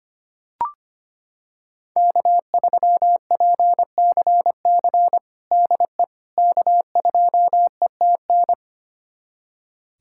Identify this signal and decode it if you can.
Morse code, heard as a keyed tone, about 25 wpm, 700 Hz. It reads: K3PCC DE K2ETN
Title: USA Callsign DE Callsign - 5 Characters - 25wpm